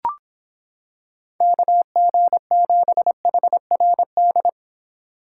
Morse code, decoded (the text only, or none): KG7HRD